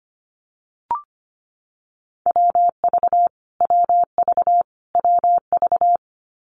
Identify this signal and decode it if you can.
Morse code, heard as a keyed tone, about 25 wpm, 700 Hz. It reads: WV WV WV
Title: US States Repeated 3x 25wpm